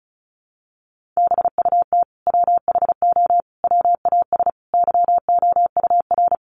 Morse code, heard as a keyed tone, about 34 wpm, 700 Hz. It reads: BUT WHO WAS YOUR
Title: Sentences from Top 200 Words 35wpm